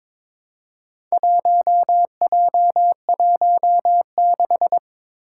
Morse code, heard as a keyed tone, about 22 wpm, 700 Hz. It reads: 1J16
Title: Four Challenging Letter-Number Combinations 22wpm